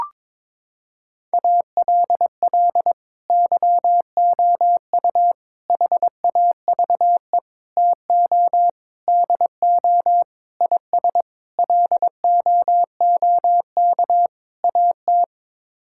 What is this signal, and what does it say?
Morse code, heard as a keyed tone, about 22 wpm, 700 Hz. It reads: ALL YOU HAVE TO DO IS LOOK AT
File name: practice / 22wpm / Sets of 8 Words - 22wpm